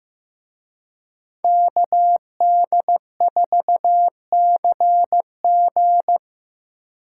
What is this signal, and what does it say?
Morse code, heard as a keyed tone, about 18 wpm, 700 Hz. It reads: KD4CG
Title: Speed Racing - USA 2x2 Call Signs 15wpm